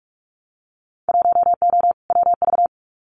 Morse code, heard as a keyed tone, about 46 wpm, 700 Hz. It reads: 1O WV
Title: Winter Field Day Exchanges 45wpm